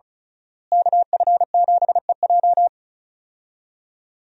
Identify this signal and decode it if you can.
Morse code, heard as a keyed tone, about 34 wpm, 700 Hz. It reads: KF7EJ
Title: USA Call Signs - 5 Characters - Repeated 3x - 35wpm